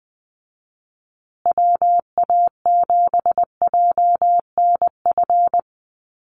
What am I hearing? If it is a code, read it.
Morse code, heard as a keyed tone, about 20 wpm, 700 Hz. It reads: WA7JNF
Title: USA Call Signs Repeated 3x 20wpm